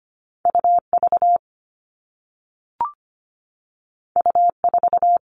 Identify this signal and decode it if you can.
Morse code, heard as a keyed tone, about 25 wpm, 700 Hz. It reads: UV U4
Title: All Cut Numbers - 2-Digit 25wpm